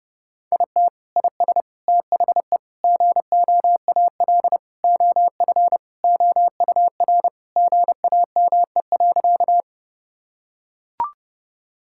Morse code, heard as a keyed tone, about 30 wpm, 700 Hz. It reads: IT IS THE GOAL OF OUR GAME.
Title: Sentences from Top 1000 Words - Review Part 5 - 30wpm